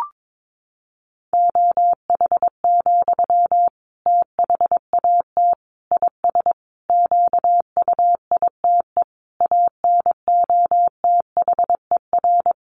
Morse code, heard as a keyed tone, about 22 wpm, 700 Hz. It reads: OH, THAT IS QUITE ANOTHER
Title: Sentences from Top 700 Words 22wpm